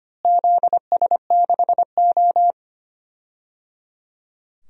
Morse code, heard as a keyed tone, about 25 wpm, 700 Hz. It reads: ZS6O
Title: DXCC-100 Call Signs - 4 Characters - Repeated 3x - 25wpm